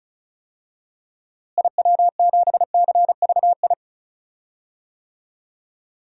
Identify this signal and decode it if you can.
Morse code, heard as a keyed tone, about 36 wpm, 700 Hz. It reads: IW7CVI
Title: DXCC-100 Call Signs - 6 Characters - Repeated 3x - 35wpm